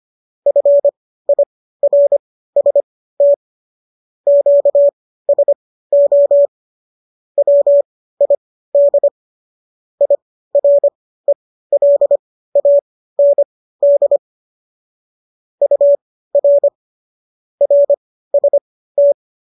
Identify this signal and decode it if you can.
Morse code, heard as a keyed tone, about 25 wpm, spaced as for 15 wpm, 570 Hz. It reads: FIRST QSO WID IRELAND UR RST